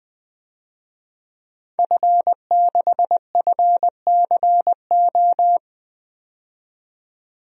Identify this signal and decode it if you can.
Morse code, heard as a keyed tone, about 20 wpm, 700 Hz. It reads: F6FCO
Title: DXCC-100 Call Signs - 5 Characters - 20wpm